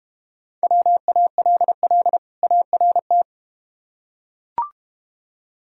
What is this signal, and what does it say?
Morse code, heard as a keyed tone, about 31 wpm, 700 Hz. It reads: WALL ART
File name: practice / 25wpm / Difficult Words - Speed-Racing with Hints - 25wpm